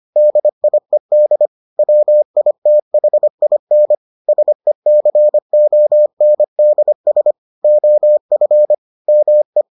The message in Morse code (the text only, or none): DIED WITHIN SECONDS OF ME